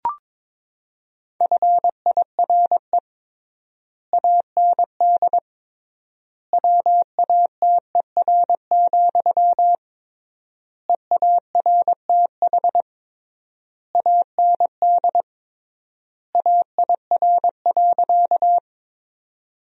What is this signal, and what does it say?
Morse code, heard as a keyed tone, about 22 wpm, 700 Hz. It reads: FIRE AND WATER, EARTH AND AIR.
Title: 3x Word-Spacing - Sentences from Top 400 Words - 22wpm